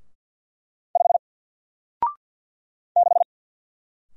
Morse code, heard as a keyed tone, about 50 wpm, 700 Hz. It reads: 5 6
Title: Single Letter-Number Alt 50wpm